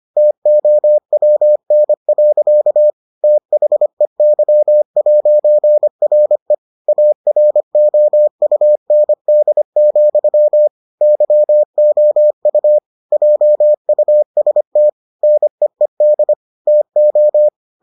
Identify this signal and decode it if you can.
Morse code, heard as a keyed tone, about 25 wpm, 600 Hz. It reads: TOWN. THEY'RE AROUND, YOU JUST NEED TO